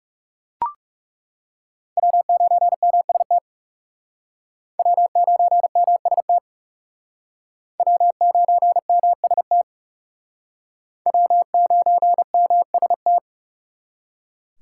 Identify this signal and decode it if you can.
Morse code, heard as a keyed tone, about 40 wpm, 700 Hz. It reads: W9MST W9MST W9MST W9MST